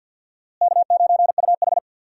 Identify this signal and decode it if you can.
Morse code, heard as a keyed tone, about 48 wpm, 700 Hz. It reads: K9UH
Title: USA Call Signs - 4 Characters - Repeated 2x - 50wpm